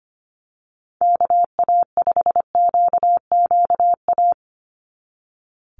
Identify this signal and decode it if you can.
Morse code, heard as a keyed tone, about 25 wpm, 700 Hz. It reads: KA5QQA
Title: NA SOTA Chaser Callsigns 25wpm